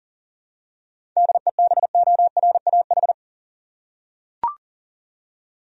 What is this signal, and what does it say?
Morse code, heard as a keyed tone, about 40 wpm, 700 Hz. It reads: DEBORAH
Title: USA Common Names - 40wpm